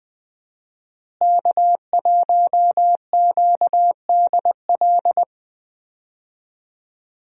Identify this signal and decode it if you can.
Morse code, heard as a keyed tone, about 20 wpm, 700 Hz. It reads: K1QDL
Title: USA Call Signs - 5 Characters - Repeated 3x - 20wpm